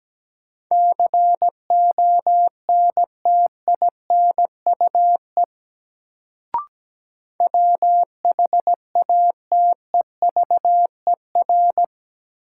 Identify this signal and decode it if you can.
Morse code, heard as a keyed tone, about 17 wpm, 700 Hz. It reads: CONTINUE WHATEVER
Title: Sentences from Top 1100 Words - New Word Drill Alt - 17wpm